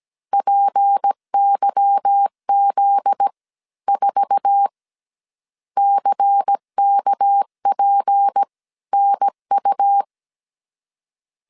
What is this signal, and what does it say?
Morse code, heard as a keyed tone, about 17 wpm, 800 Hz. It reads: PYZ 4 CKPNU
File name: CW / DET / 1 9 1500C S A 16wpm 2828